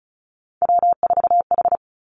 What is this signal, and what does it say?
Morse code, heard as a keyed tone, about 36 wpm, 700 Hz. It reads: W4H
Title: Three Challenging Letter-Number Combinations 35wpm